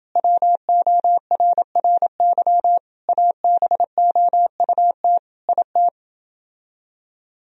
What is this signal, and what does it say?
Morse code, heard as a keyed tone, about 30 wpm, 700 Hz. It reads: WORRY ABOUT IT